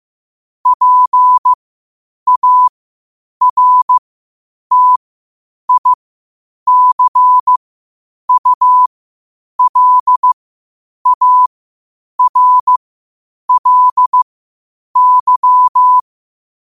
Morse code, heard as a keyed tone, about 15 wpm, spaced as for 8 wpm, 1000 Hz. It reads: PARTICULARLY
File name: qotdfiles / 8wpm